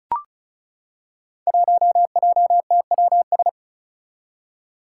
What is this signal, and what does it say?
Morse code, heard as a keyed tone, about 36 wpm, 700 Hz. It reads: 1JTWS
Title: Five Challenging Letter-Number Combinations 35wpm